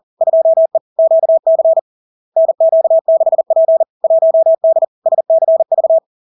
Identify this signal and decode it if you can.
Morse code, heard as a keyed tone, about 40 wpm, 650 Hz. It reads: 2E QC NQ6P 1D SCV
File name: LARK-FD-2022-Log-in-CW-40wpm-650-Hz